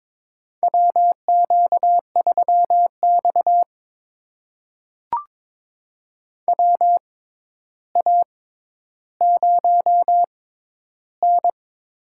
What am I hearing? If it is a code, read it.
Morse code, heard as a keyed tone, about 21 wpm, spaced as for 11 wpm, 700 Hz. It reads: WQ3X WA0N
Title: USA Call Signs - 4 Characters - 6x Character Spacing - 22wpm